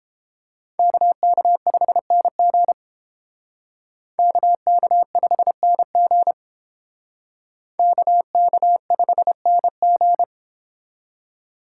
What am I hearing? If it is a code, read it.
Morse code, heard as a keyed tone, about 29 wpm, 700 Hz. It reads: KK5NG KK5NG KK5NG